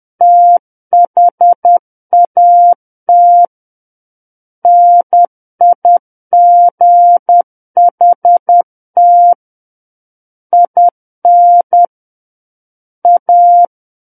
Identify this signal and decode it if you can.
Morse code, heard as a keyed tone, about 10 wpm, 700 Hz. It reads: THAT NIGHT IN A
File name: Book One / 01-Chapter 1 - 10 WPM